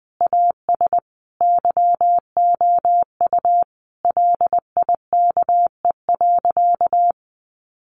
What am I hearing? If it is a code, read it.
Morse code, heard as a keyed tone, about 20 wpm, 700 Hz. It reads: AS YOU LIKE.
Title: Sentences from Top 500 Words 20wpm